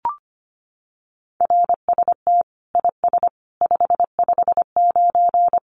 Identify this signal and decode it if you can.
Morse code, heard as a keyed tone, about 25 wpm, 700 Hz. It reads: RST IS 559
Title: Common QSO Phrases 25wpm